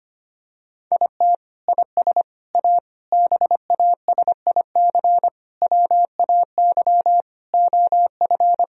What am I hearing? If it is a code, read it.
Morse code, heard as a keyed tone, about 25 wpm, 700 Hz. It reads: IT IS A BASIC WAY OF
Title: Sentences from Top 1000 Words - Review Part 3 - 25wpm